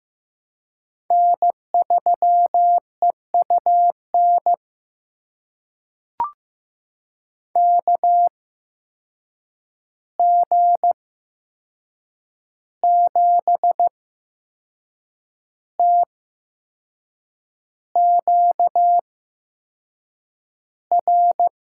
Morse code, heard as a keyed tone, about 15 wpm, spaced as for 6 wpm, 700 Hz. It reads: N3EUN KG7TQR
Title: USA Call Signs - 8x Character Spacing - 15wpm